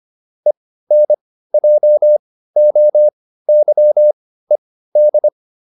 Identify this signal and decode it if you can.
Morse code, heard as a keyed tone, about 25 wpm, spaced as for 15 wpm, 600 Hz. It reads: ENJOYED